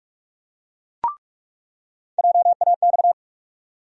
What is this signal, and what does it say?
Morse code, heard as a keyed tone, about 46 wpm, 700 Hz. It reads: JAX
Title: USA Call Sign Postfixes - 3 Letters 45wpm